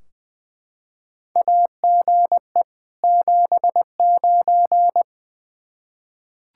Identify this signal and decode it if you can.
Morse code, heard as a keyed tone, about 20 wpm, 700 Hz. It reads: AGE 79
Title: K1USN SST QSO Practice 20wpm